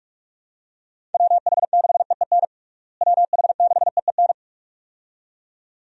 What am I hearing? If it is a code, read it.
Morse code, heard as a keyed tone, about 46 wpm, 700 Hz. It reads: WH6EEN WH6EEN
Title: USA Call Signs - 6 Characters - Repeated 2x - 45wpm